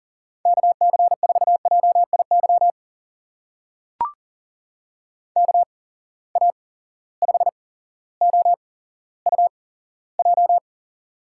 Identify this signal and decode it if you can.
Morse code, heard as a keyed tone, about 40 wpm, spaced as for 17 wpm, 700 Hz. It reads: KC4JIY KA5OUJ
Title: USA Call Signs - 6 Characters - 8x Character Spacing - 40wpm